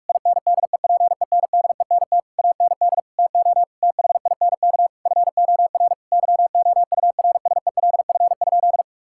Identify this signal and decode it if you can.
Morse code, heard as a keyed tone, about 41 wpm, 700 Hz. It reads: INDEPENDENT AND TO THINK FOR YOURSELF?
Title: Sentences from Top 1000 Words - Review Part 6 - 45wpm